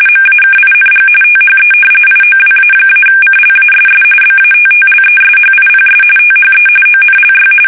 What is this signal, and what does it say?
RTTY, about 45.45 baud. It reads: MAKING THE IOWA-CLASS BATTLESHIPS THE ONLY
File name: fsk850 / battleshipnj